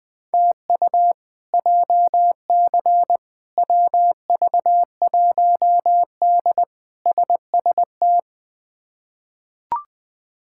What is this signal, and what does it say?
Morse code, heard as a keyed tone, about 21 wpm, 700 Hz. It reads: TU JC WV1D SST
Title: K1USN SST QSO Practice - 20wpm